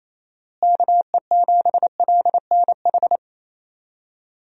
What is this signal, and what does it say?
Morse code, heard as a keyed tone, about 28 wpm, 700 Hz. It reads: KE7LNH
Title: USA Call Signs - 6 Characters - Repeated 3x - 28wpm